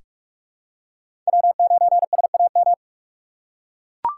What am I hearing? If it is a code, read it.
Morse code, heard as a keyed tone, about 44 wpm, 700 Hz. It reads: W9SAM
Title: USA Call Signs - 4x Character Spacing - 45wpm